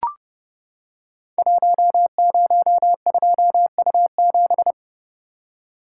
Morse code, heard as a keyed tone, about 30 wpm, 700 Hz. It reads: 102U7
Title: Code Groups - 5 Letter:Numbers - 30wpm